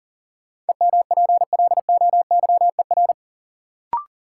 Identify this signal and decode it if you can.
Morse code, heard as a keyed tone, about 40 wpm, 700 Hz. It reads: EMPLOYER
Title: Sentences from Top 1600 Words - New Word Drill Alt - 40wpm